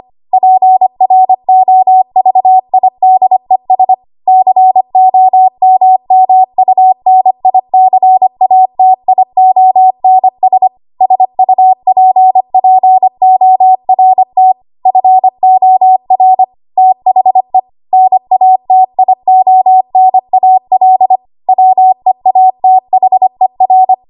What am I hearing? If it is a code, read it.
Morse code, heard as a keyed tone, about 25 wpm, 750 Hz. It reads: PROVIDES COMMUNICATIONS SUPPORT FOR THE NATIONAL WEATHER
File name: Morse / 25WPM